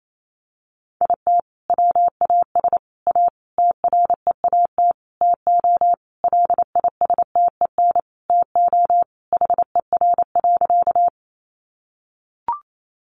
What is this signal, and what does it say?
Morse code, heard as a keyed tone, about 28 wpm, 700 Hz. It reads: IT WAS A TREAT TO LISTEN TO HER.
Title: Sentences from Top 1300 Words 28wpm